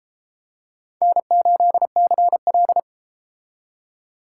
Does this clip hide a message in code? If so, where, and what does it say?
Yes: Morse code from 1.0 to 2.8 s, heard as a keyed tone, about 34 wpm, 700 Hz. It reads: N8CL